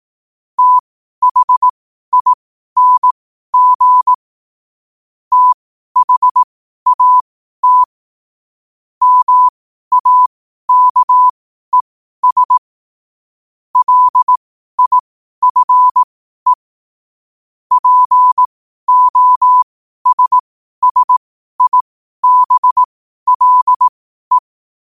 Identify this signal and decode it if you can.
Morse code, heard as a keyed tone, about 18 wpm, spaced as for 12 wpm, 1000 Hz. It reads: THING THAT MAKES LIFE POSSIBLE
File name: qotdfiles / 12wpm